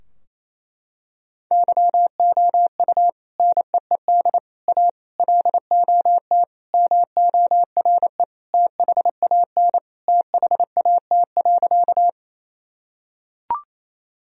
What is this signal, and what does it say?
Morse code, heard as a keyed tone, about 28 wpm, 700 Hz. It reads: YOU NEED A LOT MORE THAN THAT.
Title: Sentences from Top 1000 Words - Review Part 3 - 28wpm